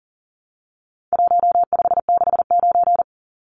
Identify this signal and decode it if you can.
Morse code, heard as a keyed tone, about 40 wpm, 700 Hz. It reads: 1569
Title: Numbers - Contest Style - 4-Digit 40wpm